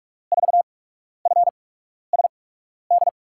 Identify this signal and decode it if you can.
Morse code, heard as a keyed tone, about 44 wpm, 700 Hz. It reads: 4 F S D
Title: USA Call Signs - 8x Character Spacing - 45wpm